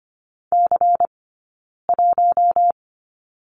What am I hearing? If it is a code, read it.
Morse code, heard as a keyed tone, about 25 wpm, 700 Hz. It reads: C 1
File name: course / 25:10wpm / 092 Lesson 16d - Letter c - Two Characters 25wpm